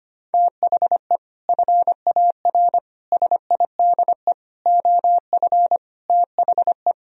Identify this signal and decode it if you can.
Morse code, heard as a keyed tone, about 32 wpm, 700 Hz. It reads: THE FAR SIDE OF THE